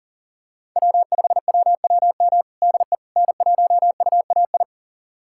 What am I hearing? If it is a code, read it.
Morse code, heard as a keyed tone, about 40 wpm, 700 Hz. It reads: W5WWM DE N1UAI
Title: USA Callsign DE Callsign - 5 Characters - 40wpm